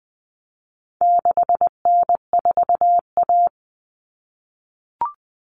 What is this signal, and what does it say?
Morse code, heard as a keyed tone, about 20 wpm, 700 Hz. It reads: 6N4A
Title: Common Cut Numbers - 4-Digit 20wpm